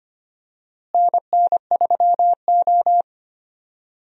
Morse code, heard as a keyed tone, about 25 wpm, 700 Hz. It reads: NN3O
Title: USA Call Signs - 4 Characters - Repeated 3x - 25wpm